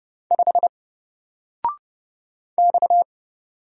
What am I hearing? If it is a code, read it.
Morse code, heard as a keyed tone, about 30 wpm, 700 Hz. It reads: 5 X
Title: Single Character (Letter:Number:Punctuation:Prosign) 30wpm